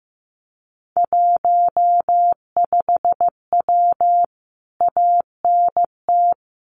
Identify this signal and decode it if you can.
Morse code, heard as a keyed tone, about 15 wpm, 700 Hz. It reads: 15W ANT
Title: Common QSO Phrases Encore 15wpm